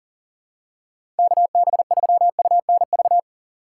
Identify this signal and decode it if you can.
Morse code, heard as a keyed tone, about 40 wpm, 700 Hz. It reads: KB3UNV
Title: USA Call Signs - 2x Character Spacing - 40wpm